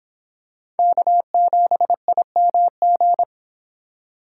Morse code, heard as a keyed tone, about 25 wpm, 700 Hz. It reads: K7IMG